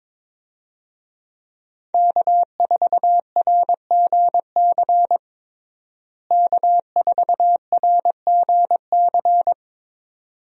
Morse code, heard as a keyed tone, about 21 wpm, 700 Hz. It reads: K4RGC K4RGC
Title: USA Call Signs - 5 Characters - Repeated 2x - 22wpm